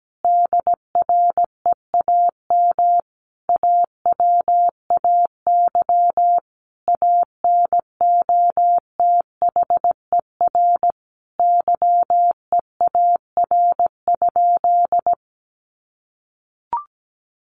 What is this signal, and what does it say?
Morse code, heard as a keyed tone, about 17 wpm, 700 Hz. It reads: DREAM AWAY ANOTHER YEAR?
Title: Sentences from Top 1500 Words 17wpm